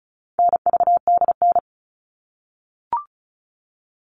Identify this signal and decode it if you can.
Morse code, heard as a keyed tone, about 34 wpm, 700 Hz. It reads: NVDN
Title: All Cut Numbers - 4-Digit 35wpm